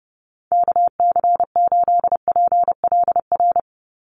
Morse code, heard as a keyed tone, about 28 wpm, 700 Hz. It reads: KC8PLR
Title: USA Call Signs 30wpm